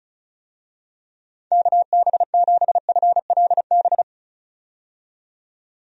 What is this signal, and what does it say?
Morse code, heard as a keyed tone, about 36 wpm, 700 Hz. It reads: KB7FLB
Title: USA Call Signs - 6 Characters - 35wpm